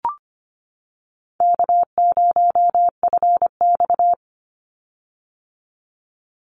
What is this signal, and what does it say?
Morse code, heard as a keyed tone, about 25 wpm, 700 Hz. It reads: K0FX
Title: Easy Call Signs - 25wpm